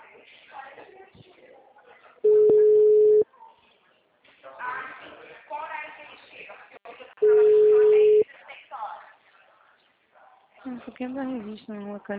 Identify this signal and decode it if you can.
Ringback tone (Europe, 425 Hz), with speech.